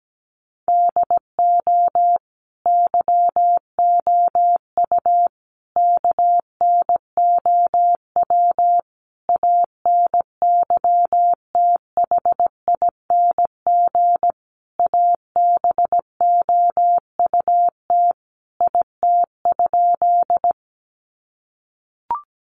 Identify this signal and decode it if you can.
Morse code, heard as a keyed tone, about 17 wpm, 700 Hz. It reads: DO YOU KNOW ANYTHING ABOUT IT?
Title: Sentences from Top 1000 Words 17wpm